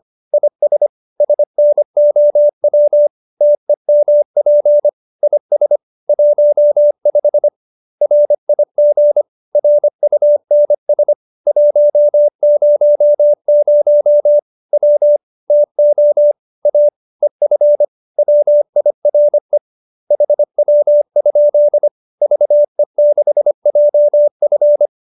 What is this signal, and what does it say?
Morse code, heard as a keyed tone, about 25 wpm, 590 Hz. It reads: IS SNOW TEMP IS 15 RIG RUNS 100 W TO A EF WIRE HW? VE6JF